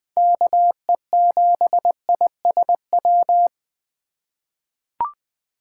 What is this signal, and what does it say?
Morse code, heard as a keyed tone, about 20 wpm, 700 Hz. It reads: KE7ISW
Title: USA Call Signs - 6 Characters - 3x Character Spacing - 20wpm